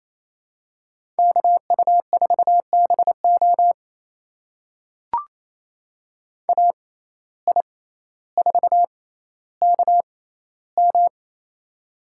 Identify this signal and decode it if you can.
Morse code, heard as a keyed tone, about 27 wpm, spaced as for 14 wpm, 700 Hz. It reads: KU4BO AI4KM
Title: USA Call Signs - 5 Characters - 6x Character Spacing - 28wpm